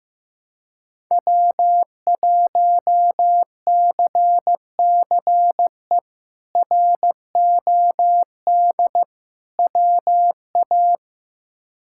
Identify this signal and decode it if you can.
Morse code, heard as a keyed tone, about 16 wpm, 700 Hz. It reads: W1CCE ROD WA
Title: K1USN SST QSO Practice - 15wpm